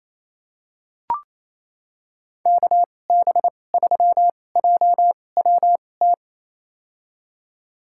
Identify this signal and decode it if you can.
Morse code, heard as a keyed tone, about 28 wpm, spaced as for 20 wpm, 700 Hz. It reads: KB3JWT
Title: USA Call Signs - 6 Characters - 2x Character Spacing - 28wpm